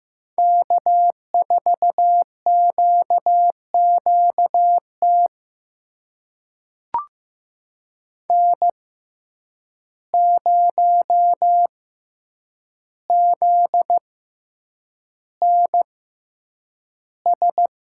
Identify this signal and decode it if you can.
Morse code, heard as a keyed tone, about 15 wpm, spaced as for 8 wpm, 700 Hz. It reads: K4QQT N0ZNS
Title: USA Call Signs - 5 Characters - 6x Character Spacing - 15wpm